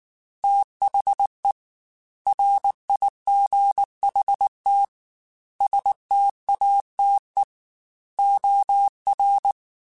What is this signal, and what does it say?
Morse code, heard as a keyed tone, about 19 wpm, 800 Hz. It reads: THE RIGHT STATE OR
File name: Technical Subjects / Technical Subjects-mp3s / OR Gate @18